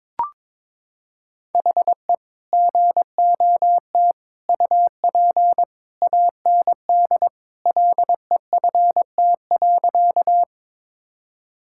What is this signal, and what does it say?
Morse code, heard as a keyed tone, about 21 wpm, 700 Hz. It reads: HE GOT UP AND LEFT.
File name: practice / 22wpm / Sentences with 2-Word Phrases - 22wpm